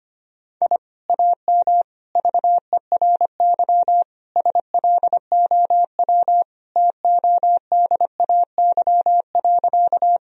Morse code, heard as a keyed tone, about 25 wpm, 700 Hz. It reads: I AM VERY SLOW TODAY.
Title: Sentences from Top 1000 Words - Review Part 2 - 25wpm